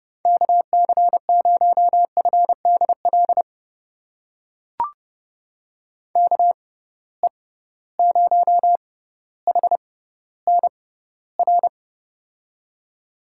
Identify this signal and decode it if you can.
Morse code, heard as a keyed tone, about 30 wpm, spaced as for 15 wpm, 700 Hz. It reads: KC0FDL KE0HNR
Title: USA Call Signs - 6 Characters - 6x Character Spacing - 30wpm